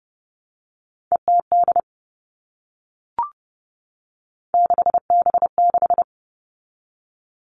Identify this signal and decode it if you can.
Morse code, heard as a keyed tone, about 30 wpm, 700 Hz. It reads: ETD 6B6
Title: All Cut Numbers - 3-Digit 30wpm